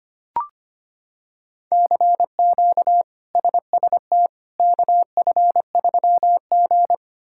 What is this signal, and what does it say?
Morse code, heard as a keyed tone, about 25 wpm, 700 Hz. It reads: CQ SST KF3G
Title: K1USN SST QSO Practice - 25wpm